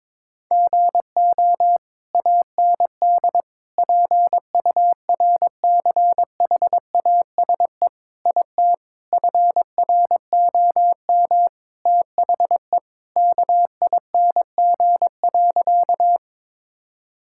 Morse code, heard as a keyed tone, about 22 wpm, 700 Hz. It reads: GO AND PURCHASE IT FROM THE KING.
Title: Sentences from Top 1000 Words - Review Part 5 - 22wpm